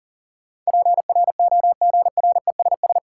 Morse code, heard as a keyed tone, about 37 wpm, 700 Hz. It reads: PROGRESS
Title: Sentences from Top 1400 Words - New Word Drill Alt - 40wpm